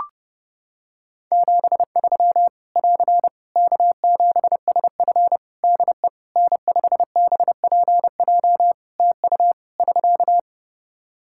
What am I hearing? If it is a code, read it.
Morse code, heard as a keyed tone, about 30 wpm, 700 Hz. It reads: 73 <AR> K7SF DE N5BPJ TU <SK>
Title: Practice QSO - Exchange 4 - Ending - 30wpm